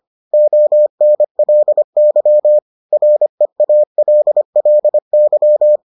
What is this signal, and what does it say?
Morse code, heard as a keyed tone, about 25 wpm, 600 Hz. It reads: ONLY REALLY